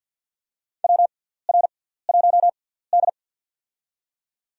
Morse code, heard as a keyed tone, about 53 wpm, spaced as for 17 wpm, 700 Hz. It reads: WR1D
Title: USA Call Signs - 6x Character Spacing - 50wpm